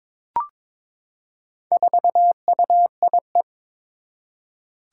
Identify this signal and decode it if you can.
Morse code, heard as a keyed tone, about 21 wpm, 700 Hz. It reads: 4UIE
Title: Four Challenging Letter-Number Combinations 22wpm